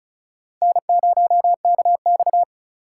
Morse code, heard as a keyed tone, about 46 wpm, 700 Hz. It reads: N0KX